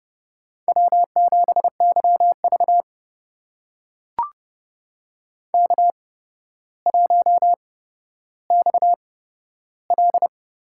Morse code, heard as a keyed tone, about 30 wpm, spaced as for 13 wpm, 700 Hz. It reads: W7YV K1XL
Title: USA Call Signs - 4 Characters - 8x Character Spacing - 30wpm